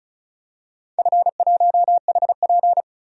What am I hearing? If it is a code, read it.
Morse code, heard as a keyed tone, about 36 wpm, 700 Hz. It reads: F1HP
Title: DXCC-100 Call Signs - 4 Characters - Repeated 3x - 35wpm